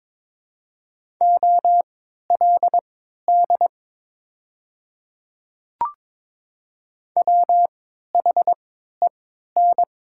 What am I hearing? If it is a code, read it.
Morse code, heard as a keyed tone, about 23 wpm, spaced as for 13 wpm, 700 Hz. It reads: OLD WHEN
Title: 3x Character Spacing - Top 200 Words - 22wpm